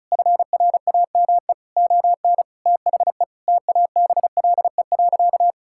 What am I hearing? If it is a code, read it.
Morse code, heard as a keyed tone, about 36 wpm, 700 Hz. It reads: FRAME ON THE TABLE.
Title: Sentences from Top 1000 Words - Review Part 6 - 35wpm